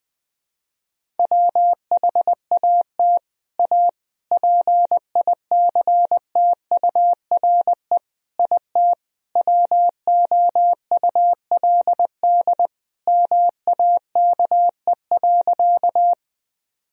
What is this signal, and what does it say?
Morse code, heard as a keyed tone, about 20 wpm, 700 Hz. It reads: WHAT A PICTURE IT WOULD MAKE.
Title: Sentences from Top 1000 Words - Review Part 1 - 20wpm